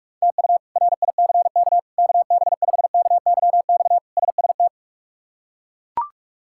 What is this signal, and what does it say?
Morse code, heard as a keyed tone, about 41 wpm, 700 Hz. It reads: TU RICK KB5KYX SST
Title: K1USN SST QSO Practice - 45wpm